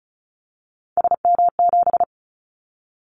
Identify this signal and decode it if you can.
Morse code, heard as a keyed tone, about 34 wpm, 700 Hz. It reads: SM7
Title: Callsign Prefixes for Top 25 DXCC - 35wpm